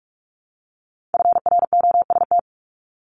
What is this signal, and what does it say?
Morse code, heard as a keyed tone, about 46 wpm, 700 Hz. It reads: FROST
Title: Multi-letter Phonemes Repeated 3x 45wpm